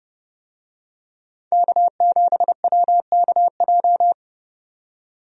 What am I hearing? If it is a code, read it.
Morse code, heard as a keyed tone, about 30 wpm, 700 Hz. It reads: K7WKJ
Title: USA Call Signs - 5 Characters - 30wpm